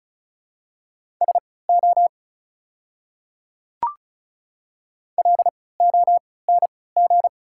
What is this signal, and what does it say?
Morse code, heard as a keyed tone, about 34 wpm, spaced as for 20 wpm, 700 Hz. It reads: SO LONG
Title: 3x Character Spacing - Top 100 Words - 35wpm